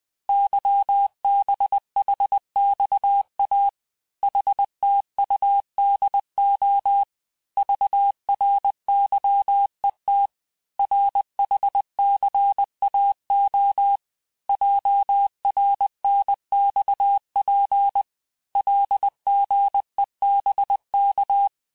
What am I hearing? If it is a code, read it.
Morse code, heard as a keyed tone, about 20 wpm, 800 Hz. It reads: YBHXA HTUDO VRYET RHCAO JRNXP LGEBK
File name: cours-1584-teste-chiffre-vitesse-19mpm